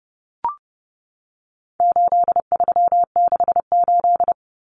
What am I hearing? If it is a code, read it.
Morse code, heard as a keyed tone, about 30 wpm, 700 Hz. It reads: 8368
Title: Numbers - 4-Digit 30wpm